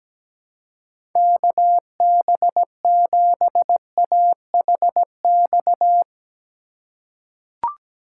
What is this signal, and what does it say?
Morse code, heard as a keyed tone, about 17 wpm, 700 Hz. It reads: KB7AHX
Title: USA Call Signs - 6 Characters - 4x Character Spacing - 17wpm